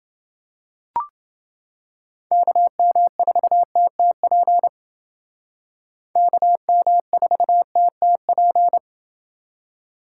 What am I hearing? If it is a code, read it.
Morse code, heard as a keyed tone, about 29 wpm, 700 Hz. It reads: KM4TTP KM4TTP